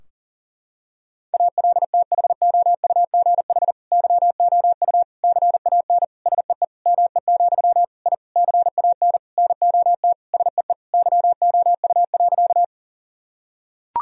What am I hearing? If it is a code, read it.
Morse code, heard as a keyed tone, about 40 wpm, 700 Hz. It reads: ALTHOUGH YOU CAN SEE ME, I CAN NOT SEE YOU.
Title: Sentences from Top 1000 Words - Review Part 3 - 40wpm